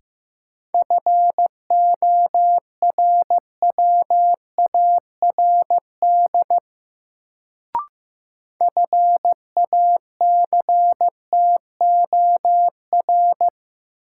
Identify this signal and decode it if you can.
Morse code, heard as a keyed tone, about 15 wpm, 700 Hz. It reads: FORWARD FACTOR
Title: Sentences from Top 1300 Words - New Word Drill Alt - 15wpm